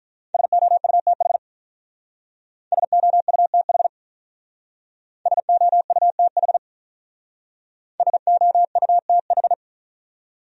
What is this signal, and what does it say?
Morse code, heard as a keyed tone, about 42 wpm, 700 Hz. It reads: SOUTH SOUTH SOUTH SOUTH
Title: Speed-Racing Top 500 Words 35wpm